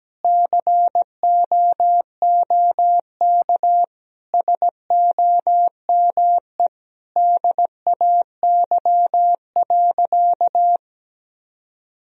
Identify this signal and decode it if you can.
Morse code, heard as a keyed tone, about 17 wpm, 700 Hz. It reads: COOK SOME DAY.
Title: Sentences from Top 1000 Words - Review Part 4 - 17wpm